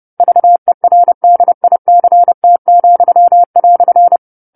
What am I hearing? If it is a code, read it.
Morse code, heard as a keyed tone, about 30 wpm, 700 Hz. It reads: VERDICT,"
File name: 10-Chapter-10 30 WPM